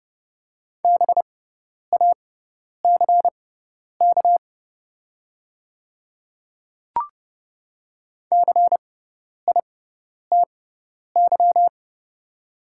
Morse code, heard as a keyed tone, about 30 wpm, spaced as for 10 wpm, 700 Hz. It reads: BACK CITY
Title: 6x Character Spacing - Top 300 Words - 30wpm